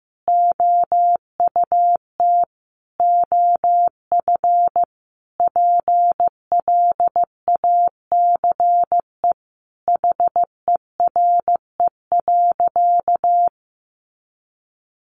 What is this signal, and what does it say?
Morse code, heard as a keyed tone, about 15 wpm, 700 Hz. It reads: OUT OF PLACE HERE.
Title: Sentences from Top 1500 Words 15wpm